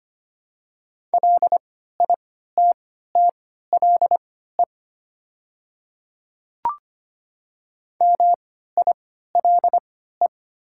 Morse code, heard as a keyed tone, about 25 wpm, spaced as for 14 wpm, 700 Hz. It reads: LITTLE MILE
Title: 3x Character Spacing - Top 300 Words - 25wpm